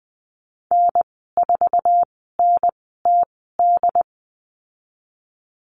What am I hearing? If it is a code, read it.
Morse code, heard as a keyed tone, about 19 wpm, spaced as for 14 wpm, 700 Hz. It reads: N4NTD
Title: 088 Lesson 15g - Number 5 - Call Signs 20wpm